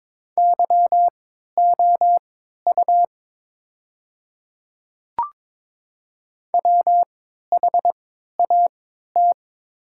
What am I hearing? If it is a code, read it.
Morse code, heard as a keyed tone, about 22 wpm, spaced as for 13 wpm, 700 Hz. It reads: YOU WHAT
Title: 3x Character Spacing - Top 100 Words - 22wpm